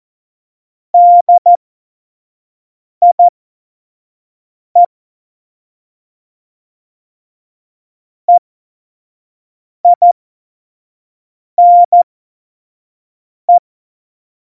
Morse code, heard as a keyed tone, about 14 wpm, spaced as for 5 wpm, 700 Hz. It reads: DIE EINE
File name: Morsekurs / Witze / Witz 10.Juli 2018, 25 F70